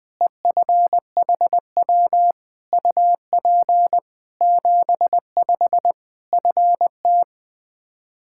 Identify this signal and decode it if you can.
Morse code, heard as a keyed tone, about 20 wpm, 700 Hz. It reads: EFHW UP 75 FT
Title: Practice QSO - Exchange 2 - Info - 20wpm